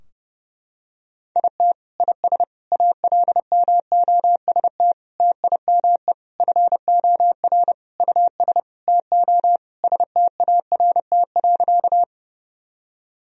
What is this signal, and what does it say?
Morse code, heard as a keyed tone, about 32 wpm, 700 Hz. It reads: IT IS ALMOST TIME FOR US TO START.
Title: Sentences from Top 1000 Words - Review Part 4 - 30wpm